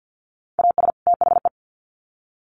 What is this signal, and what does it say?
Morse code, heard as a keyed tone, about 50 wpm, 700 Hz. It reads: AS THE
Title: Sets of 2 Words 50wpm